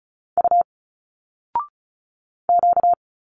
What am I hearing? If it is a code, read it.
Morse code, heard as a keyed tone, about 36 wpm, 700 Hz. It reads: U Q
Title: Single Letters 35wpm